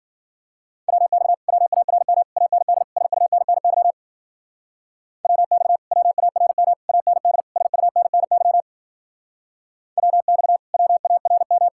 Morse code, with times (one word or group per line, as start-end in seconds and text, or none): 0.9-1.3 WX
1.5-2.2 WARM
2.4-2.8 AND
3.0-3.9 SUNNY
5.2-5.8 WX
5.9-6.7 WARM
6.9-7.4 AND
7.6-8.6 SUNNY
10.0-10.6 WX
10.7-11.7 WARM